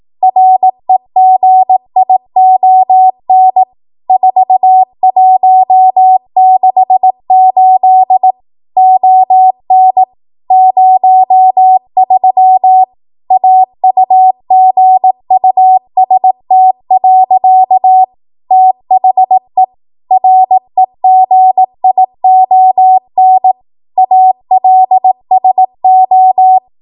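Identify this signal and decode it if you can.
Morse code, heard as a keyed tone, about 18 wpm, 750 Hz. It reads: REGION 4168 ON 03 AUGUST. THE REGION ALSO
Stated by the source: ARRL 18 wpm Morse code practice archive transmission.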